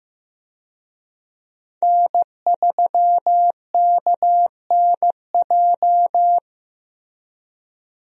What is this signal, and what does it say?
Morse code, heard as a keyed tone, about 16 wpm, 700 Hz. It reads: N3KNJ
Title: USA Call Signs - 5 Characters - Repeated 2x - 15wpm